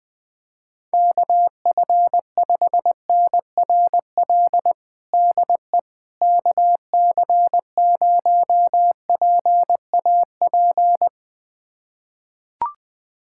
Morse code, heard as a keyed tone, about 21 wpm, 700 Hz. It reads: KF5NRL DE KC0PAP
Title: USA Callsign DE Callsign 20wpm